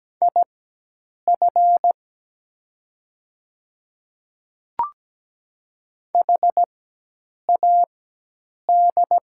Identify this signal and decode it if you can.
Morse code, heard as a keyed tone, about 17 wpm, spaced as for 8 wpm, 700 Hz. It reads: IF HAD
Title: 4x Character Spacing - Top 100 Words - 17wpm